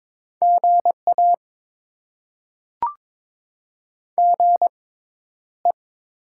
Morse code, heard as a keyed tone, about 22 wpm, spaced as for 11 wpm, 700 Hz. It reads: GA GE
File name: Top 20 QSO Elements - 6x Character Spacing - 22wpm